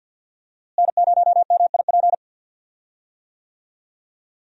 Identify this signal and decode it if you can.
Morse code, heard as a keyed tone, about 51 wpm, 700 Hz. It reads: N0MIP
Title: USA Call Signs - 5 Characters - Repeated 2x - 50wpm